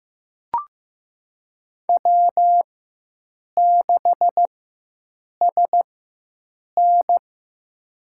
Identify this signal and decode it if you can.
Morse code, heard as a keyed tone, about 15 wpm, spaced as for 7 wpm, 700 Hz. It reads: W6SN
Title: USA Call Signs - 4 Characters - 4x Character Spacing - 15wpm